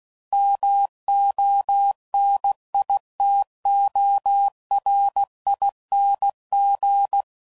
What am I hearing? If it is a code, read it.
Morse code, heard as a keyed tone, about 16 wpm, 800 Hz. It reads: MONITORING
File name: Suffixes / Special INGs / Sngl Ings